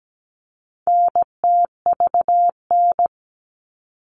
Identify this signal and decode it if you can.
Morse code, heard as a keyed tone, about 17 wpm, 700 Hz. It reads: NTVN
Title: All Cut Numbers - 4-Digit 17wpm